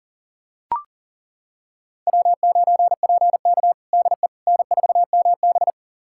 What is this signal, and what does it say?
Morse code, heard as a keyed tone, about 40 wpm, 700 Hz. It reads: W9PK DE N4MB
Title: USA Callsign DE Callsign - 4 Characters - 40wpm